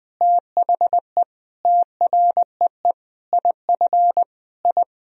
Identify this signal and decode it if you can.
Morse code, heard as a keyed tone, about 20 wpm, 700 Hz. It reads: THE TREE IF I
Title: Sentences from Top 1000 Words - Review Part 1 - 20wpm